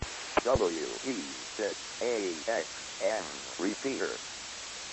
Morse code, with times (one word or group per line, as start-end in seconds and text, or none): none